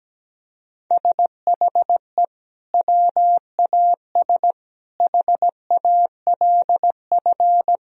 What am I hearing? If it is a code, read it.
Morse code, heard as a keyed tone, about 17 wpm, 700 Hz. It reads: SHE WAS HALF
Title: Sentences from Top 1000 Words - Review Part 2 - 17wpm